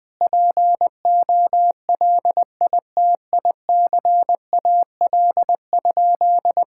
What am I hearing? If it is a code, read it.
Morse code, heard as a keyed tone, about 20 wpm, 700 Hz. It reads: POLITICAL?
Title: Sentences from Top 1000 Words - Review Part 5 - 20wpm